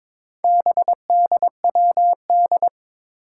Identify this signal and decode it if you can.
Morse code, heard as a keyed tone, about 22 wpm, 700 Hz. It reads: BDWD
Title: Four Challenging Letter-Number Combinations 22wpm